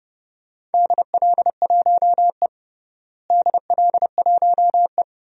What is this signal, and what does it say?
Morse code, heard as a keyed tone, about 30 wpm, 700 Hz. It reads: DL1E DL1E
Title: DXCC-100 Call Signs - 4 Characters - Repeated 2x - 30wpm